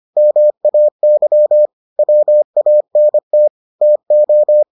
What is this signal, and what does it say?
Morse code, heard as a keyed tone, about 25 wpm, 600 Hz. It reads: MAY WANT TO